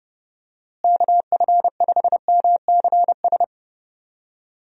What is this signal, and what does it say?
Morse code, heard as a keyed tone, about 32 wpm, 700 Hz. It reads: KF5MCS
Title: USA Call Signs - 6 Characters - Repeated 2x - 30wpm